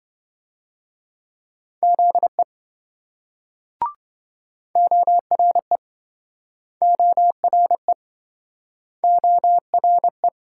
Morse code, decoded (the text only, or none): ZE ORE ORE ORE